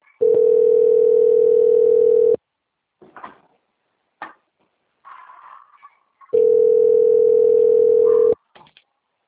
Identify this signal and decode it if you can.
Ringback tone (North America), 440 + 480 Hz.
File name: out- - - - - .16266